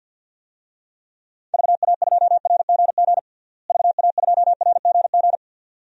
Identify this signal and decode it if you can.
Morse code, heard as a keyed tone, about 49 wpm, 700 Hz. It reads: VA2RGG VA2RGG
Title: DXCC-100 Call Signs - 6 Characters - Repeated 2x - 50wpm